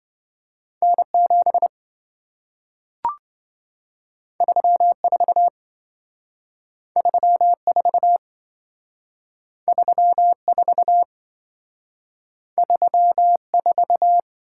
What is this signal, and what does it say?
Morse code, heard as a keyed tone, about 28 wpm, 700 Hz. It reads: N7 34 34 34 34